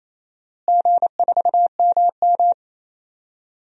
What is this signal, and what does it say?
Morse code, heard as a keyed tone, about 28 wpm, 700 Hz. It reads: G4MM
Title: DXCC-100 Call Signs - 4 Characters - Repeated 2x - 28wpm